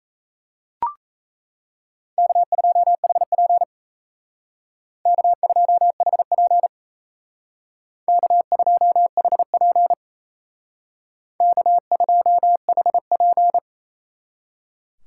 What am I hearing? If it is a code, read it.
Morse code, heard as a keyed tone, about 34 wpm, 700 Hz. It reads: K2HP K2HP K2HP K2HP
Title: Speed Racing - USA Call Signs 28wpm